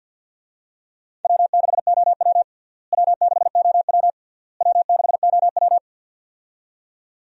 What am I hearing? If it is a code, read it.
Morse code, heard as a keyed tone, about 47 wpm, 700 Hz. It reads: W6OW W6OW W6OW
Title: USA Call Signs - 4 Characters - Repeated 3x - 50wpm